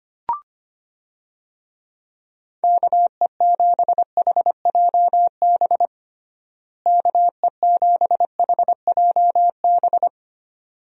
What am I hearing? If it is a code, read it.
Morse code, heard as a keyed tone, about 25 wpm, 700 Hz. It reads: KE7HJB KE7HJB
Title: USA Call Signs - 6 Characters - Repeated 2x - 25wpm